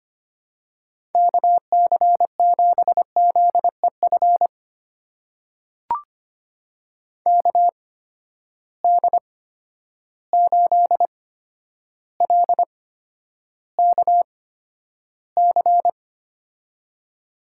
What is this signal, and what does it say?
Morse code, heard as a keyed tone, about 25 wpm, spaced as for 11 wpm, 700 Hz. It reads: KC7ZEF KD8LKC
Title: USA Call Signs - 6 Characters - 8x Character Spacing - 25wpm